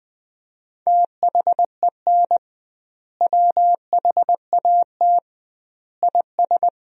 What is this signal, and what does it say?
Morse code, heard as a keyed tone, about 20 wpm, 700 Hz. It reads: THEN WHAT IS
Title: 2x Word-Spacing - Sentences from Top 100 Words - 20wpm